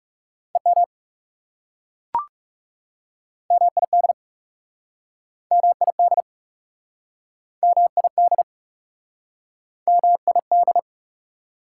Morse code, heard as a keyed tone, about 40 wpm, 700 Hz. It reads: EM MID MID MID MID